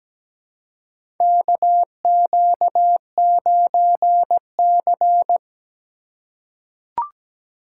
Morse code, heard as a keyed tone, about 17 wpm, 700 Hz. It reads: KQ9C
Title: USA Call Signs - 4 Characters - 2x Character Spacing - 17wpm